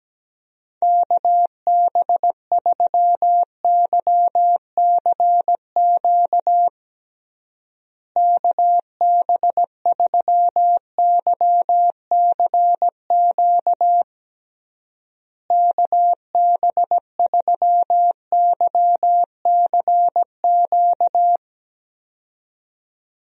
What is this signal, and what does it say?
Morse code, heard as a keyed tone, about 17 wpm, 700 Hz. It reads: KB3YCQ KB3YCQ KB3YCQ
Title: USA Call Signs - 6 Characters - Repeated 3x - 17wpm